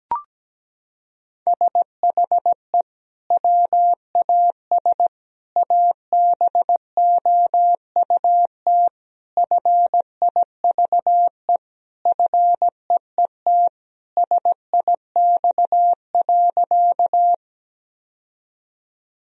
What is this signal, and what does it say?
Morse code, heard as a keyed tone, about 17 wpm, 700 Hz. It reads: SHE WAS ABOUT FIVE FEET SIX.
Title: Sentences from Top 1000 Words - Review Part 2 - 17wpm